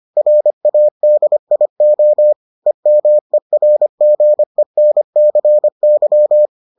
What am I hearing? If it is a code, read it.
Morse code, heard as a keyed tone, about 25 wpm, 600 Hz. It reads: RADIO EMERGENCY